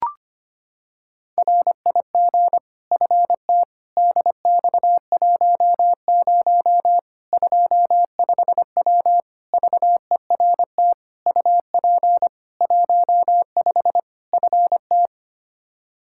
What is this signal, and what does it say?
Morse code, heard as a keyed tone, about 25 wpm, 700 Hz. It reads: RIG FT DX10 25W VERT UP 15 FT
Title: K1USN SST QSO Practice 25wpm